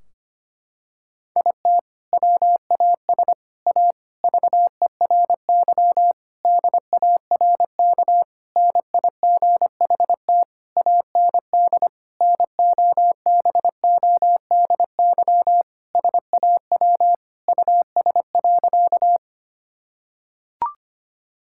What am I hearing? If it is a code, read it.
Morse code, heard as a keyed tone, about 25 wpm, 700 Hz. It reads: IT WAS A VERY DARK NIGHT AND NOBODY SAW US.
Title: Sentences from Top 1600 Words 25wpm